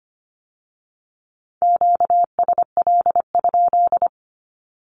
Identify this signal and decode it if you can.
Morse code, heard as a keyed tone, about 25 wpm, 700 Hz. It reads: QSL?
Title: Q-Codes with Spoken Meaning - 25wpm